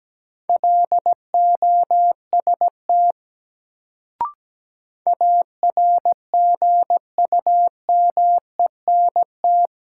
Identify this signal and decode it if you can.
Morse code, heard as a keyed tone, about 17 wpm, 700 Hz. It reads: LOST ARGUMENT
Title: Sentences from Top 1400 Words - New Word Drill - 17wpm